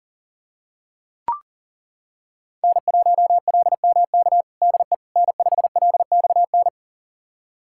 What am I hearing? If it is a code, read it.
Morse code, heard as a keyed tone, about 40 wpm, 690 Hz. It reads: N1LMK DE N5LXN
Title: USA Callsign DE Callsign - 5 Characters - 40wpm